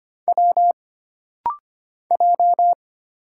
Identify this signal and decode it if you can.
Morse code, heard as a keyed tone, about 25 wpm, 700 Hz. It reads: W J
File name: Single Character (Letter:Number:Punctuation:Prosign) - Rapid-Fire - 25wpm